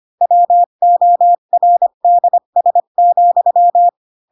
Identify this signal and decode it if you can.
Morse code, heard as a keyed tone, about 25 wpm, 700 Hz. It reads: WORDS,
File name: Pinocchio / Pinocchio ch2 20 25 ik1hge